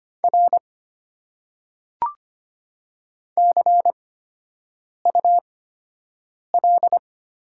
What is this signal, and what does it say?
Morse code, heard as a keyed tone, about 25 wpm, spaced as for 7 wpm, 700 Hz. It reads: R CUL
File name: Top 90 QSO Elements - 8x Character Spacing - 25wpm